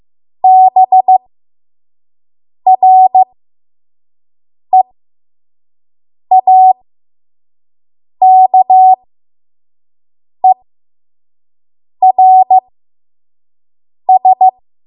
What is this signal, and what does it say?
Morse code, heard as a keyed tone, about 15 wpm, spaced as for 5 wpm, 750 Hz. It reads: BREAKERS
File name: Morse / 05WPM